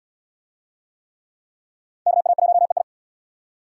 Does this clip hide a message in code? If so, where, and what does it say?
Yes: Morse code from 2.1 to 2.8 s, heard as a keyed tone, about 65 wpm, 690 Hz. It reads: KA1EI